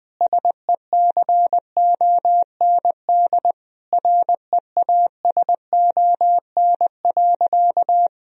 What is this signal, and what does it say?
Morse code, heard as a keyed tone, about 20 wpm, 700 Hz. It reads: SECOND REASON.
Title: Sentences from Top 1000 Words - Review Part 1 - 20wpm